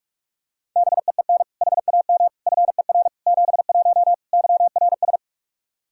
Morse code, heard as a keyed tone, about 46 wpm, 700 Hz. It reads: BEEN HAM FER 71 YRS
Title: Practice QSO - Exchange 3 - More Info - 45wpm